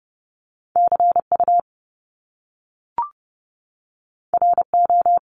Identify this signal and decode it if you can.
Morse code, heard as a keyed tone, about 30 wpm, 700 Hz. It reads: CU RO
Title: Top Two Letter Combinations in Words 30wpm